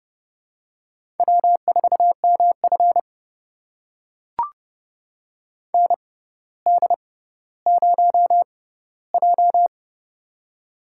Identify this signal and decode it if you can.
Morse code, heard as a keyed tone, about 32 wpm, spaced as for 16 wpm, 700 Hz. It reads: W4MF ND0J
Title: USA Call Signs - 4 Characters - 6x Character Spacing - 30wpm